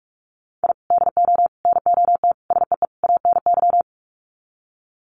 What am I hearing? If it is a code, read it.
Morse code, heard as a keyed tone, about 44 wpm, 700 Hz. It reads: I DO NOT SEE ANY
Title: Sets of 5 Words 45wpm